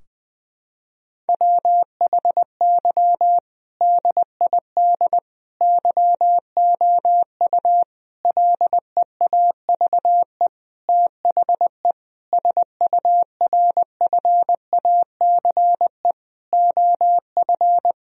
Morse code, heard as a keyed tone, about 20 wpm, 700 Hz. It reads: WHY DID YOU LEAVE THE SURFACE OF
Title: Sentences from Top 1000 Words - Review Part 3 - 20wpm